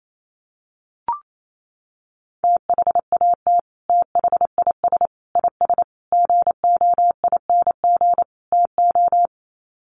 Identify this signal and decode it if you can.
Morse code, heard as a keyed tone, about 28 wpm, 700 Hz. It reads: THAT THIS IS GOING TO
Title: Sets of 5 Words Encore2 28wpm